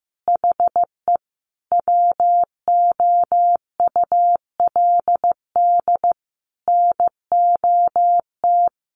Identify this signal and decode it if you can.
Morse code, heard as a keyed tone, about 15 wpm, 700 Hz. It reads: HE WOULD NOT
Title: Sentences from Top 800 Words 15wpm